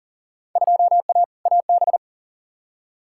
Morse code, heard as a keyed tone, about 37 wpm, 700 Hz. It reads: 2A AB
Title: Field Day Exchanges - 40wpm